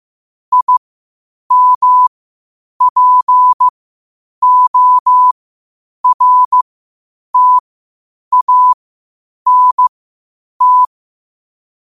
Morse code, heard as a keyed tone, about 15 wpm, spaced as for 8 wpm, 1000 Hz. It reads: IMPORTANT
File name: qotdfiles / 8wpm